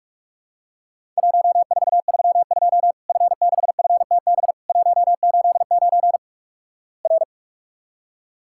Morse code, heard as a keyed tone, about 48 wpm, 700 Hz. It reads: 1432 F6FTB 189 R
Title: Worked All Europe DX Contest - QTC Exchange - 45wpm